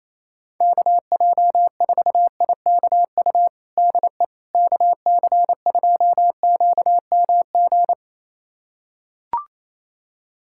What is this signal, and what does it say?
Morse code, heard as a keyed tone, about 28 wpm, 700 Hz. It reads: KJ4IKU DE KC2QMG
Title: USA Callsign DE Callsign - 6 Characters - 28wpm